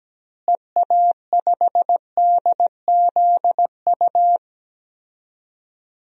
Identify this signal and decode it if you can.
Morse code, heard as a keyed tone, about 17 wpm, 700 Hz. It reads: EA5DZU
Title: DXCC-100 Call Signs - 6 Characters - Repeated 3x - 17wpm